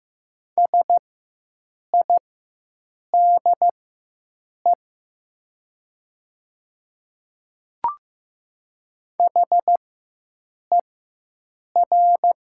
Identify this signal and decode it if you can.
Morse code, heard as a keyed tone, about 15 wpm, spaced as for 7 wpm, 700 Hz. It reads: SIDE HER
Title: 4x Character Spacing - Top 100 Words - 15wpm